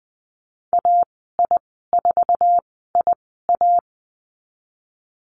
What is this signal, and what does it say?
Morse code, heard as a keyed tone, about 20 wpm, spaced as for 15 wpm, 700 Hz. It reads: AI4IA
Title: 048 Lesson 9g - Number 4 - Call Signs 20wpm